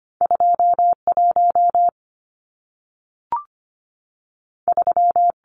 Morse code, heard as a keyed tone, about 25 wpm, 700 Hz. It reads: 21 3
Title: Numbers - 2-Digit 25wpm